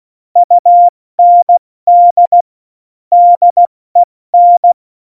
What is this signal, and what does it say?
Morse code, heard as a keyed tone, about 16 wpm, spaced as for 14 wpm, 700 Hz. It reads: UND DEN
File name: Morsekurs / Witze / Witz 16. Januar 2018, 70 F80